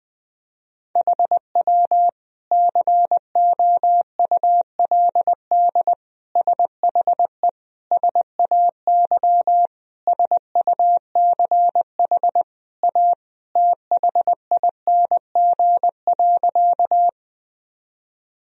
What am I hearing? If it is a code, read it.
Morse code, heard as a keyed tone, about 21 wpm, 700 Hz. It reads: HW COULD SHE SAY SUCH A THING.
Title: Sentences from Top 200 Words and Abbreviations 20wpm